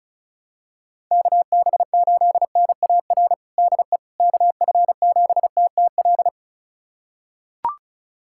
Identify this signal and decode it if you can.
Morse code, heard as a keyed tone, about 36 wpm, 700 Hz. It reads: KB8NAR DE KF7TTL
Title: USA Callsign DE Callsign - 6 Characters - 35wpm